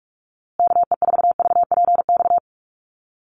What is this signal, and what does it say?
Morse code, heard as a keyed tone, about 44 wpm, 700 Hz. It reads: KE4VPX
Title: USA Call Signs Repeated 2x 45wpm